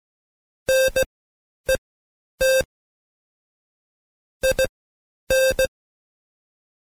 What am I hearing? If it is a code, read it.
Morse code, heard as a keyed tone, about 17 wpm, spaced as for 9 wpm, 530 Hz. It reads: NET IN
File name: Code Quick / AudioDownld2 / Seg131-Stepper4 9WPM